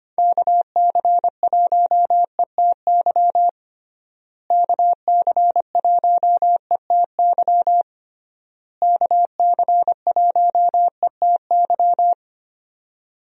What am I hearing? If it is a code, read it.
Morse code, heard as a keyed tone, about 25 wpm, 700 Hz. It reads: KC1ETY KC1ETY KC1ETY
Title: USA Call Signs - 6 Characters - Repeated 3x - 25wpm